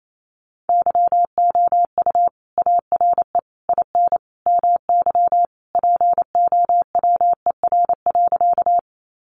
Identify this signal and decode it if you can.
Morse code, heard as a keyed tone, about 28 wpm, 700 Hz. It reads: YOU ARE IN MY POWER.
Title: Sentences from Top 500 Words 28wpm